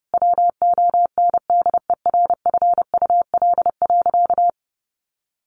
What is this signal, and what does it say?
Morse code, heard as a keyed tone, about 30 wpm, 700 Hz. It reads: WONDERFUL.
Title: Sentences from Top 1100 Words 30wpm